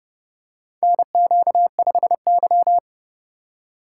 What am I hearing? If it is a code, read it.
Morse code, heard as a keyed tone, about 30 wpm, 700 Hz. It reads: NQ5Y
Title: USA Call Signs - 4 Characters - 4x Character Spacing - 30wpm